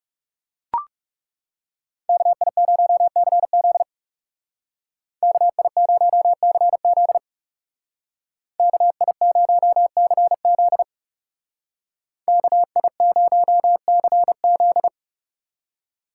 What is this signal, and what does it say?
Morse code, heard as a keyed tone, about 35 wpm, 700 Hz. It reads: KI0CZ KI0CZ KI0CZ KI0CZ
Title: Speed Racing - USA 2x2 Call Signs 30wpm